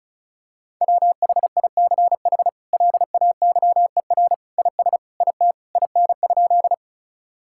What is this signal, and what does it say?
Morse code, heard as a keyed tone, about 36 wpm, 700 Hz. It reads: WHICH LAYER IS IT IN?
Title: Sentences from Top 1600 Words 35wpm